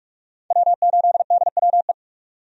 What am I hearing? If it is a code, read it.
Morse code, heard as a keyed tone, about 46 wpm, 700 Hz. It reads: W8DWE
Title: USA Call Signs - 5 Characters - 45wpm